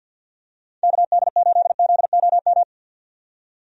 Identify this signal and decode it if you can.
Morse code, heard as a keyed tone, about 48 wpm, 700 Hz. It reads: KD8ZOM
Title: USA Call Signs - 6 Characters - 50wpm